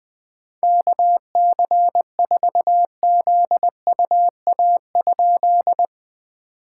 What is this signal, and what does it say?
Morse code, heard as a keyed tone, about 20 wpm, 690 Hz. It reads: KC4ZUA?
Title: POTA QSO Practice - 20wpm